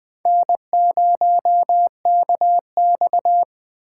Morse code, heard as a keyed tone, about 21 wpm, 700 Hz. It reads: N0KX